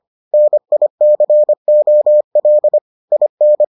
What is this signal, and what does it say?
Morse code, heard as a keyed tone, about 25 wpm, 600 Hz. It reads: NICOL IN